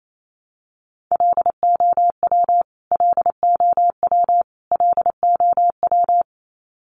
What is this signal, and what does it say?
Morse code, heard as a keyed tone, about 28 wpm, 700 Hz. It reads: LOW LOW LOW
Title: Top 500 Words Repeated 28wpm